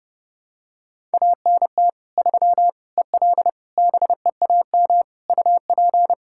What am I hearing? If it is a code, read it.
Morse code, heard as a keyed tone, about 30 wpm, 700 Hz. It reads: ANT 3 EL BEAM UP
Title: Practice QSO - Exchange 2 - Info - 30wpm